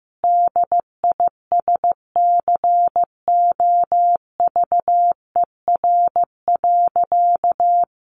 Morse code, heard as a keyed tone, about 15 wpm, 700 Hz. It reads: DISCOVER.
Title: Sentences from Top 1100 Words 15wpm